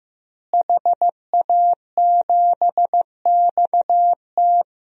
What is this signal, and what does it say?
Morse code, heard as a keyed tone, about 15 wpm, 700 Hz. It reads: HA7XT
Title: DXCC-100 Call Signs - 5 Characters - Repeated 2x - 15wpm